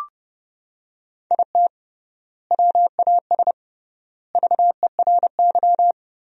Morse code, heard as a keyed tone, about 28 wpm, 700 Hz. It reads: IT WAS VERY
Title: 3x Word-Spacing - Sentences from Top 500 Words - 30wpm